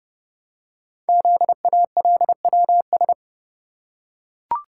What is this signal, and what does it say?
Morse code, heard as a keyed tone, about 29 wpm, 700 Hz. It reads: ZALWS
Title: Code Groups - 5 Letters - 30wpm